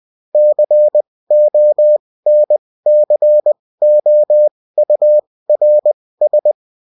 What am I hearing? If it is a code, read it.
Morse code, heard as a keyed tone, about 20 wpm, spaced as for 16 wpm, 600 Hz. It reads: CONCOURS